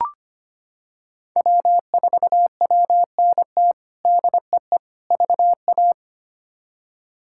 Morse code, heard as a keyed tone, about 25 wpm, 700 Hz. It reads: W4WNT DEE VA
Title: K1USN SST QSO Practice - 25wpm